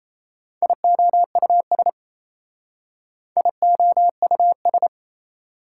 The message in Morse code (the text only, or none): IOUS IOUS